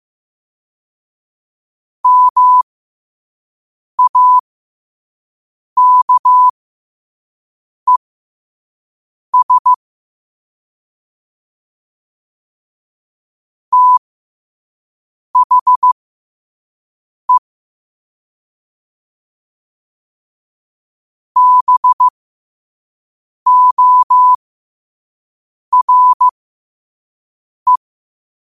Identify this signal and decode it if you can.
Morse code, heard as a keyed tone, about 15 wpm, spaced as for 5 wpm, 1000 Hz. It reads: MAKES THE BORE
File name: qotdfiles / 5wpm